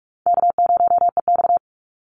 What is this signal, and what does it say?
Morse code, heard as a keyed tone, about 44 wpm, 700 Hz. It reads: K0EX
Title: Call Signs - 45wpm